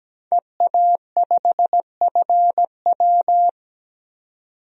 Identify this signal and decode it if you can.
Morse code, heard as a keyed tone, about 17 wpm, 700 Hz. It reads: EA5FW
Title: DXCC-100 Call Signs - 5 Characters - 17wpm